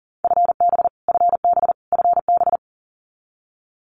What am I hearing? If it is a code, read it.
Morse code, heard as a keyed tone, about 40 wpm, 700 Hz. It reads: FB FB FB
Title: Abbreviations Repeated 3x - 40wpm